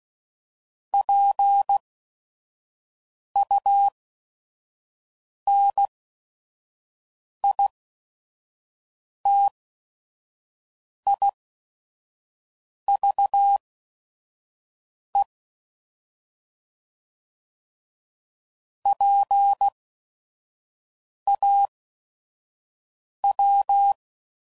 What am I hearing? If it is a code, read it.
Morse code, heard as a keyed tone, about 16 wpm, spaced as for 5 wpm, 800 Hz. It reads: PUNITIVE PAW
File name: Prefixes / By First Letter / Ps